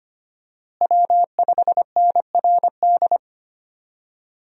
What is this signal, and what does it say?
Morse code, heard as a keyed tone, about 25 wpm, 700 Hz. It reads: W5NRD
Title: USA Call Signs - 5 Characters - Repeated 2x - 25wpm